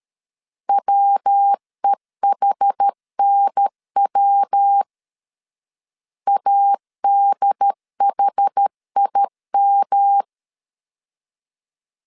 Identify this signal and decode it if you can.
Morse code, heard as a keyed tone, about 13 wpm, 800 Hz. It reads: WEHNW ADHIM
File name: CW / DET / 1 3 1500C S B 12wpm 2828